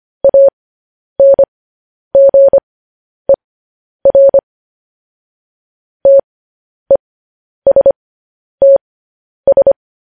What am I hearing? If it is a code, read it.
Morse code, heard as a keyed tone, about 25 wpm, spaced as for 10 wpm, 550 Hz. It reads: ANGER TESTS
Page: News Headlines in Morse Code at 10 WPM